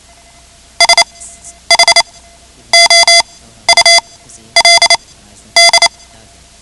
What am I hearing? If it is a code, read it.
Morse code, heard as a keyed tone, about 28 wpm, spaced as for 11 wpm, 2050 Hz. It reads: SHOULD
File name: ham radio / CodeQuick / Code Quick 10